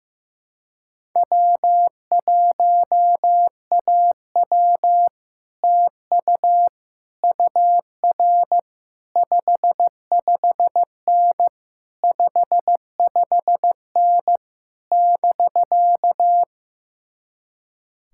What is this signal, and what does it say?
Morse code, heard as a keyed tone, about 15 wpm, 700 Hz. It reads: W1AW TU UR 55N 55N <BK>
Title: Speed-Racing Common QSO Phrases Encore - 15wpm